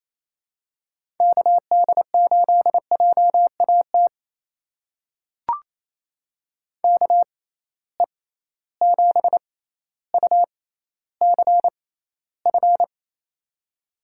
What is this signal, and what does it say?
Morse code, heard as a keyed tone, about 28 wpm, spaced as for 14 wpm, 700 Hz. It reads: KD8JAT KE7UCF
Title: USA Call Signs - 6x Character Spacing - 28wpm